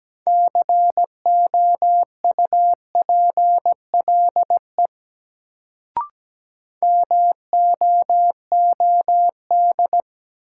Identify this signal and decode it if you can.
Morse code, heard as a keyed tone, about 17 wpm, 700 Hz. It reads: COUPLE MOOD
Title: Sentences from Top 1100 Words - New Word Drill - 17wpm